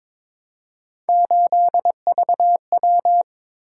Morse code, heard as a keyed tone, about 22 wpm, 700 Hz. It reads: 8VW
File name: practice / 22wpm / Three Letter-Number Combinations - 22wpm